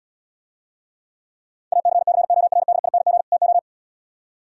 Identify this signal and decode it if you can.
Morse code, heard as a keyed tone, about 80 wpm, 700 Hz. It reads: ACCORDING TO